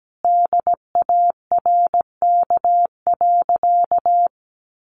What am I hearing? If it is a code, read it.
Morse code, heard as a keyed tone, about 17 wpm, 700 Hz. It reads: DARK.
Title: Sentences from Top 500 Words 17wpm